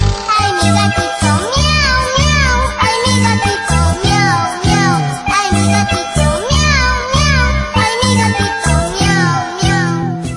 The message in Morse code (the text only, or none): none